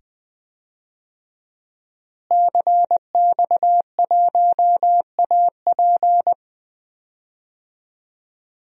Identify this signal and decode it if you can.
Morse code, heard as a keyed tone, about 20 wpm, 700 Hz. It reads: CX1AP
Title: DXCC-100 Call Signs - 5 Characters - Repeated 2x - 20wpm